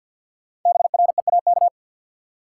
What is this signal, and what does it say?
Morse code, heard as a keyed tone, about 53 wpm, 700 Hz. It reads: BREAK
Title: Sentences from Top 900 Words - New Word Drill - 50wpm